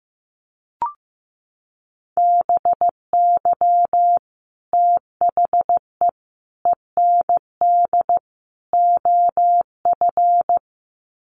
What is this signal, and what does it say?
Morse code, heard as a keyed tone, about 15 wpm, 700 Hz. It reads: BY THE END OF
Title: Sets of 4 Words 15wpm